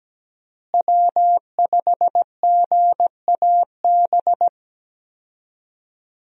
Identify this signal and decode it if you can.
Morse code, heard as a keyed tone, about 17 wpm, 700 Hz. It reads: W5GAB
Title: USA Call Signs - 5 Characters - Repeated 2x - 17wpm